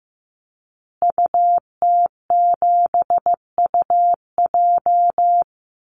Morse code, heard as a keyed tone, about 15 wpm, 700 Hz. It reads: UT7UJ
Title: Call Signs Repeated 2x - 15wpm